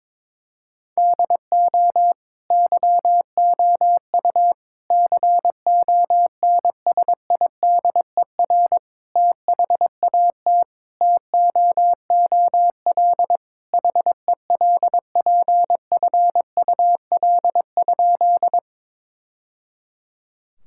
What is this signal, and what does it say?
Morse code, heard as a keyed tone, about 22 wpm, 700 Hz. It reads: DO YOU CONSIDER THAT TOOL HELPFUL?
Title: Sentences from Top 1000 Words - Review Part 6 - 22wpm